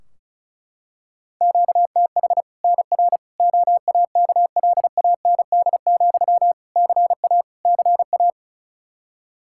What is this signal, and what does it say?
Morse code, heard as a keyed tone, about 36 wpm, 700 Hz. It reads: QTH NR OAKLAND, CA CA
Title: K1USN SST QSO Practice 35wpm